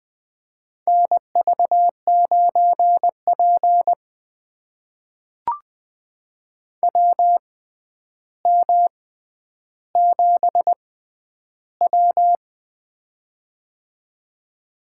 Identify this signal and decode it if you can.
Morse code, heard as a keyed tone, about 20 wpm, spaced as for 10 wpm, 700 Hz. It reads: NV9P WM7W
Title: USA Call Signs - 4 Characters - 6x Character Spacing - 20wpm